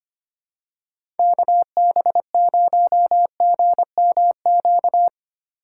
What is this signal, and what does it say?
Morse code, heard as a keyed tone, about 25 wpm, 700 Hz. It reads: KB0GMQ
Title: USA Call Signs - 6 Characters - 3x Character Spacing - 25wpm